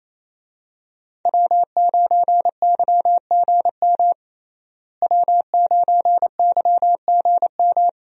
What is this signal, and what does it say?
Morse code, heard as a keyed tone, about 28 wpm, 700 Hz. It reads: W9YGM W9YGM
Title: USA Call Signs - 5 Characters - Repeated 3x - 28wpm